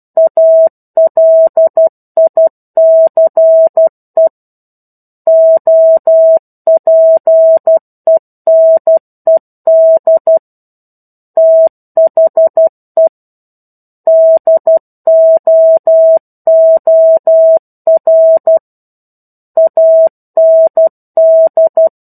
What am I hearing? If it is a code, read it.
Morse code, heard as a keyed tone, about 12 wpm, 650 Hz. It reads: ALICE OPENED THE DOOR AND
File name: 01-Chapter-01 12 WPM